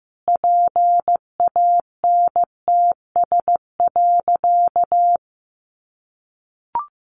Morse code, heard as a keyed tone, about 15 wpm, 700 Hz. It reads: PANTS.
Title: Additional Easy Phrases 15wpm